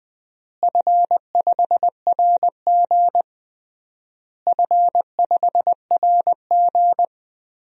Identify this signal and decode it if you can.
Morse code, heard as a keyed tone, about 20 wpm, 700 Hz. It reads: F5RG F5RG
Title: DXCC-100 Call Signs - 4 Characters - Repeated 3x - 20wpm